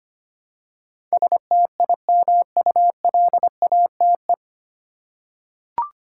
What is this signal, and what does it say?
Morse code, heard as a keyed tone, about 25 wpm, 700 Hz. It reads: STIMULATE
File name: Nine Letter Words - 25wpm